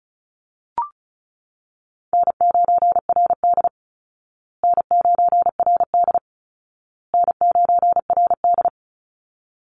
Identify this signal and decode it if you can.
Morse code, heard as a keyed tone, about 34 wpm, 700 Hz. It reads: N9RD N9RD N9RD
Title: Easy Call Signs Repeated 3x - 35wpm